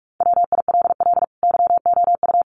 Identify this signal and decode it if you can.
Morse code, heard as a keyed tone, about 44 wpm, 700 Hz. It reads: WILL YOU
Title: Sentences from Top 900 Words 45wpm